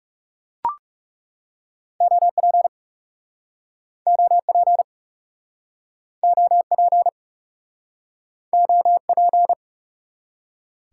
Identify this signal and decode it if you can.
Morse code, heard as a keyed tone, about 41 wpm, 700 Hz. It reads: OP OP OP OP